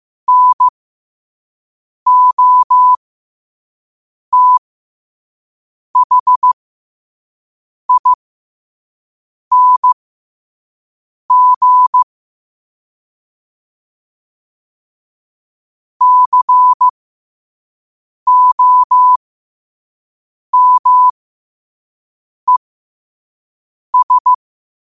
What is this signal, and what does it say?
Morse code, heard as a keyed tone, about 15 wpm, spaced as for 5 wpm, 1000 Hz. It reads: NOTHING COMES
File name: qotdfiles / 5wpm